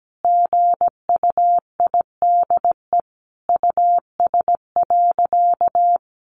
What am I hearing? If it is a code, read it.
Morse code, heard as a keyed tone, about 17 wpm, 700 Hz. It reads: GUIDE US.
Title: Sentences from Top 800 Words 17wpm